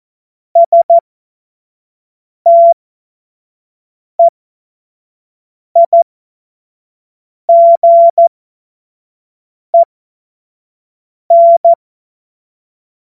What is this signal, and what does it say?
Morse code, heard as a keyed tone, about 14 wpm, spaced as for 5 wpm, 680 Hz. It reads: STEIGEN
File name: Morsekurs / Witze / Witz 11. April 2023 25 F70